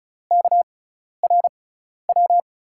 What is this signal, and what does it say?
Morse code, heard as a keyed tone, about 36 wpm, spaced as for 12 wpm, 700 Hz. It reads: KRW
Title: USA Call Signs - 5 Characters - 6x Character Spacing - 35wpm